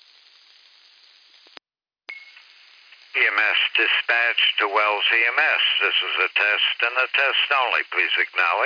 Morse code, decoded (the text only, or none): none